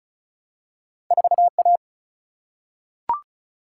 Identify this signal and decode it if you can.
Morse code, heard as a keyed tone, about 36 wpm, 700 Hz. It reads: 4A
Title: Single-Letter Single-Number 35wpm